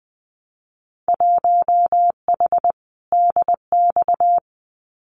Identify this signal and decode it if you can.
Morse code, heard as a keyed tone, about 20 wpm, 700 Hz. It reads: 1H DX
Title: Winter Field Day Exchanges 20wpm